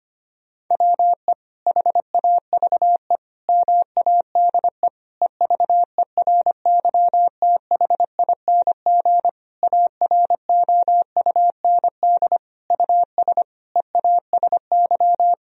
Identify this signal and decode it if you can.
Morse code, heard as a keyed tone, about 25 wpm, 700 Hz. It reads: WE HAVE MADE EVERYTHING AROUND US EASY
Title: Sentences from Top 1000 Words - Review Part 3 - 25wpm